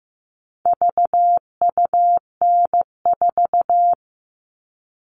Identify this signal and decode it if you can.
Morse code, heard as a keyed tone, about 15 wpm, 700 Hz. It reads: VUN4
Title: All Cut Numbers - 4-Digit 15wpm